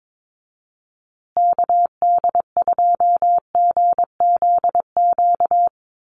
Morse code, heard as a keyed tone, about 22 wpm, 700 Hz. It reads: KD2GZQ
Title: USA Call Signs Repeated 3x 22wpm